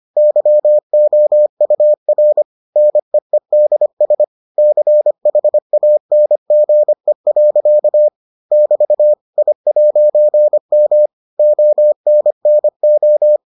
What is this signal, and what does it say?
Morse code, heard as a keyed tone, about 25 wpm, 600 Hz. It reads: YOUR NEEDS CHANGE. <BT> I'M ONNO